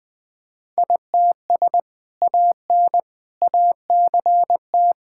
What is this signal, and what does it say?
Morse code, heard as a keyed tone, about 20 wpm, 700 Hz. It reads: ITS AN ACT
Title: Sentences from Top 1000 Words - Review Part 5 - 20wpm